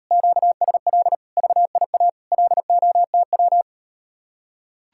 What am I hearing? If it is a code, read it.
Morse code, heard as a keyed tone, about 38 wpm, 700 Hz. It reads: QSL VIA LOTW